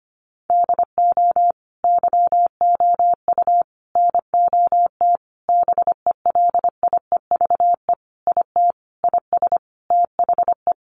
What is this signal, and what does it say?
Morse code, heard as a keyed tone, about 25 wpm, 700 Hz. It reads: DO YOU NOT BELIEVE IT IS THE
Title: Sentences from Top 1000 Words 25wpm